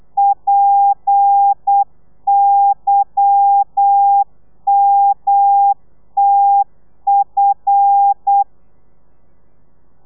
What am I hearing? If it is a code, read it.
Morse code, heard as a keyed tone, about 8 wpm, 800 Hz. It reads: PYMTF
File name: telegrafia / i2tfj / Lez10 35